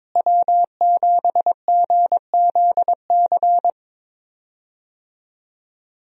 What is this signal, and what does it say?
Morse code, heard as a keyed tone, about 22 wpm, 700 Hz. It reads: W7GZC
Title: USA Call Signs - 5 Characters - Repeated 2x - 22wpm